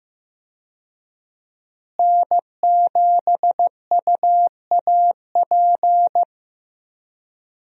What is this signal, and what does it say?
Morse code, heard as a keyed tone, about 15 wpm, 700 Hz. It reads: N7UAP
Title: USA Call Signs - 5 Characters - Repeated 3x - 15wpm